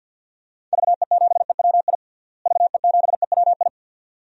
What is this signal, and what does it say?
Morse code, heard as a keyed tone, about 52 wpm, 700 Hz. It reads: VE7EWI VE7EWI
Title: DXCC-100 Call Signs - 6 Characters - Repeated 2x - 50wpm